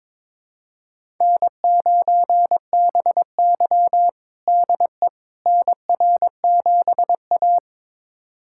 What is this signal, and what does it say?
Morse code, heard as a keyed tone, about 21 wpm, 700 Hz. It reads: N9BY DE NR7A
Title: USA Callsign DE Callsign - 4 Characters - 22wpm